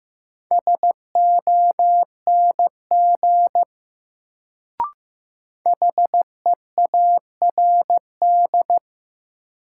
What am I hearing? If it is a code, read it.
Morse code, heard as a keyed tone, about 15 wpm, 700 Hz. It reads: SONG HEARD
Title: Sentences from Top 400 Words - New Word Drill - 15wpm